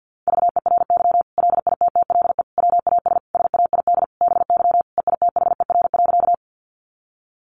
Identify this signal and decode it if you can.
Morse code, heard as a keyed tone, about 50 wpm, 700 Hz. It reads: VERY LITTLE WAS SAID BY EITHER.
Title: Sentences from Top 600 Words 50wpm